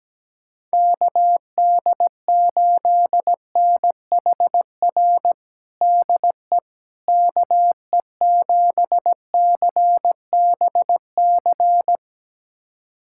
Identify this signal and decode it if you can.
Morse code, heard as a keyed tone, about 17 wpm, 700 Hz. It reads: KD8NHR DE KE7CBC
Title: USA Callsign DE Callsign 17wpm